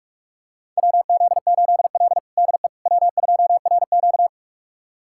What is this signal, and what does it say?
Morse code, heard as a keyed tone, about 46 wpm, 700 Hz. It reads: WZ8L DE W2RQ
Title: USA Callsign DE Callsign - 4 Characters - 45wpm